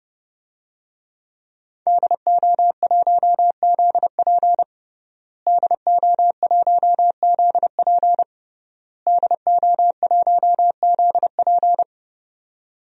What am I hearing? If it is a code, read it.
Morse code, heard as a keyed tone, about 30 wpm, 700 Hz. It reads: DO1ZP DO1ZP DO1ZP
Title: DXCC-100 Call Signs - 5 Characters - Repeated 3x - 30wpm